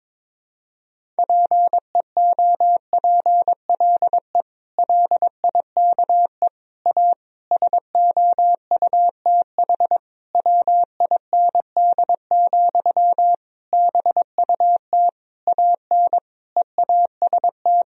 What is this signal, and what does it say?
Morse code, heard as a keyed tone, about 22 wpm, 700 Hz. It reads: PEOPLE LIKE A SOUTH WIND, BUT AN EAST
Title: Sentences from Top 1000 Words - Review Part 3 - 22wpm